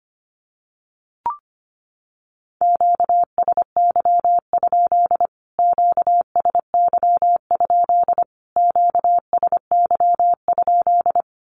Morse code, heard as a keyed tone, about 25 wpm, 700 Hz. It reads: QSY? QSY? QSY?
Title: Q-Codes Repeated 3x 25wpm